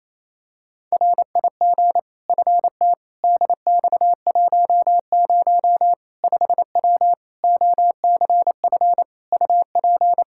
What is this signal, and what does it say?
Morse code, heard as a keyed tone, about 28 wpm, 700 Hz. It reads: RIG FT DX10 5W OCF UP
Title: Practice QSO - Exchange 2 - Info - 28wpm